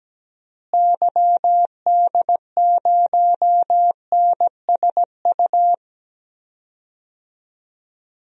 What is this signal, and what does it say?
Morse code, heard as a keyed tone, about 17 wpm, 700 Hz. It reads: YD0NSU
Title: DXCC-100 Call Signs - 6 Characters - Repeated 3x - 17wpm